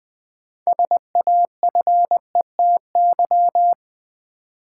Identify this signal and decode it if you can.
Morse code, heard as a keyed tone, about 20 wpm, 700 Hz. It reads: SAFETY
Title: 6 Letter Words - First 3 Letters Given - 20wpm